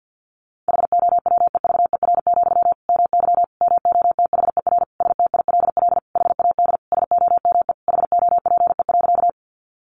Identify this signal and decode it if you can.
Morse code, heard as a keyed tone, about 48 wpm, 700 Hz. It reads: HOWEVER, MY MOTHER STILL HAD SOME HOPE.
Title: Sentences from Top 1000 Words 50wpm